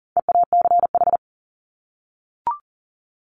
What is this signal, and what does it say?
Morse code, heard as a keyed tone, about 40 wpm, 700 Hz. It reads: EACH
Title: Top 300 Words 40wpm